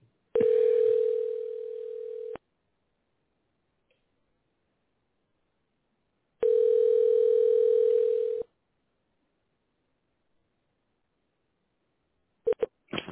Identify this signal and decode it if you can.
Ringback tone (North America), 440 + 480 Hz.